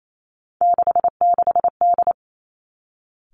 Morse code, heard as a keyed tone, about 28 wpm, 700 Hz. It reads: 66D
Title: Three Challenging Letter-Number Combinations 28wpm